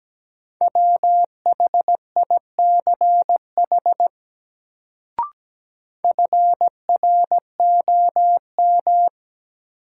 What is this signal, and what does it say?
Morse code, heard as a keyed tone, about 17 wpm, 700 Hz. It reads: WHICH FROM
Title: Sentences from Top 100 Words - New Word Drill - 17wpm